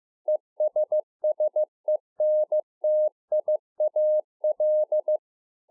Morse code, heard as a keyed tone, about 15 wpm, 610 Hz. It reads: ESSENTIAL